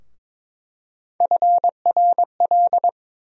Morse code, heard as a keyed tone, about 22 wpm, 700 Hz. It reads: FRL
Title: USA Call Sign Postfixes - 3 Letters 22wpm